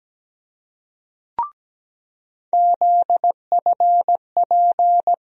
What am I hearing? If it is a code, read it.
Morse code, heard as a keyed tone, about 17 wpm, 700 Hz. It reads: ZFP
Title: USA Call Sign Postfixes - 3 Letters 17wpm